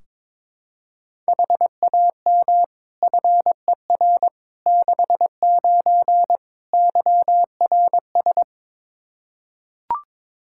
Morse code, heard as a keyed tone, about 22 wpm, 700 Hz. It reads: HAM FER 69 YRS
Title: K1USN SST QSO Practice 22wpm